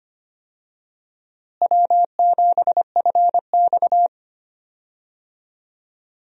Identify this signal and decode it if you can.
Morse code, heard as a keyed tone, about 25 wpm, 700 Hz. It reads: W7FX
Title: USA Call Signs - 4 Characters - Repeated 3x - 25wpm